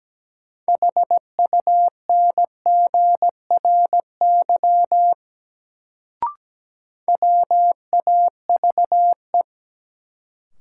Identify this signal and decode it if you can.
Morse code, heard as a keyed tone, about 17 wpm, 700 Hz. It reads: HUNGRY WAVE
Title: Sentences from Top 1400 Words - New Word Drill Alt - 17wpm